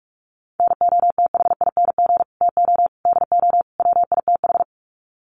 Morse code, heard as a keyed tone, about 44 wpm, 700 Hz. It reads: NOTHING TO DO WITH
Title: Sets of 4 Words 45wpm